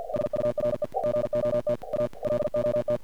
Morse code, heard as a keyed tone, about 50 wpm, 600 Hz, with noise. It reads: SURE GOT A LOT
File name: archived / CRSnet